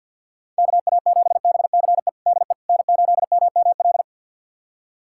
Morse code, heard as a keyed tone, about 48 wpm, 700 Hz. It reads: KA7BCE DE N8MML
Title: USA Callsign DE Callsign 50wpm